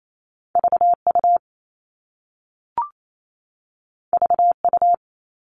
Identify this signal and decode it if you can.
Morse code, heard as a keyed tone, about 28 wpm, 700 Hz. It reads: VU VU
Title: All Cut Numbers - 2-Digit 28wpm